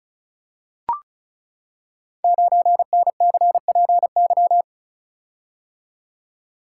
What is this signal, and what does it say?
Morse code, heard as a keyed tone, about 36 wpm, 690 Hz. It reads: 9NCPY
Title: Code Groups - 5 Letter:Numbers - 35wpm